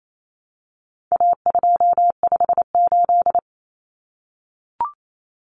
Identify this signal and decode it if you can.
Morse code, heard as a keyed tone, about 28 wpm, 700 Hz. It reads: A258
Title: Common Cut Numbers - 4-Digit 28wpm